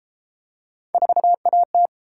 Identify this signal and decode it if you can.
Morse code, heard as a keyed tone, about 30 wpm, 700 Hz. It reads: 4AT